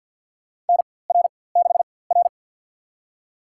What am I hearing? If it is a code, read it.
Morse code, heard as a keyed tone, about 52 wpm, spaced as for 24 wpm, 690 Hz. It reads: NR6R
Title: USA Call Signs - 4x Character Spacing - 50wpm